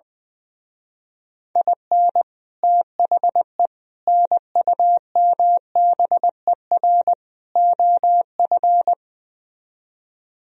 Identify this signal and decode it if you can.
Morse code, heard as a keyed tone, about 20 wpm, 700 Hz. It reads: IN THE NUMBER OF
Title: Speed-Racing - Sets of 4 Words 20wpm